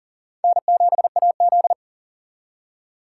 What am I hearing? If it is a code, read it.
Morse code, heard as a keyed tone, about 39 wpm, 700 Hz. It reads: N7AZ
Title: K1USN SST QSO Practice - 40wpm